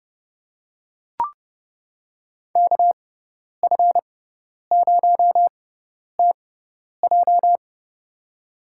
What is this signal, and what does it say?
Morse code, heard as a keyed tone, about 30 wpm, spaced as for 10 wpm, 700 Hz. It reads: KF0TJ
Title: USA Call Signs - 5 Characters - 6x Character Spacing - 30wpm